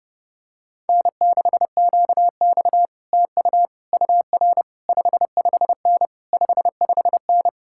Morse code, heard as a keyed tone, about 29 wpm, 700 Hz. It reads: N6QX TU UR 55N 55N